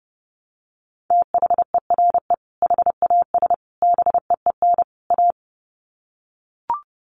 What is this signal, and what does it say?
Morse code, heard as a keyed tone, about 30 wpm, 700 Hz. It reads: THERE HAS BEEN A
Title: Sets of 4 Words 30wpm